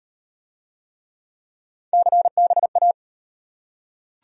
Morse code, heard as a keyed tone, about 35 wpm, 700 Hz. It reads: CBA